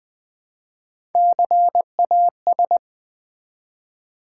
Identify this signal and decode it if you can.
Morse code, heard as a keyed tone, about 20 wpm, 700 Hz. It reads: CAS
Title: Build A Word - 1 Letter At A Time - 4-Letter Words - 20wpm